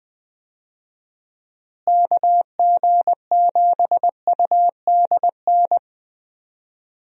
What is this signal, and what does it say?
Morse code, heard as a keyed tone, about 19 wpm, 700 Hz. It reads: KG7UDN
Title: USA Call Signs - 6 Characters - Repeated 3x - 20wpm